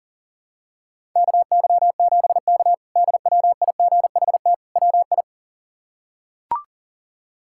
Morse code, heard as a keyed tone, about 40 wpm, 700 Hz. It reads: KY7K DWIGHT WI
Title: K1USN SST QSO Practice - 40wpm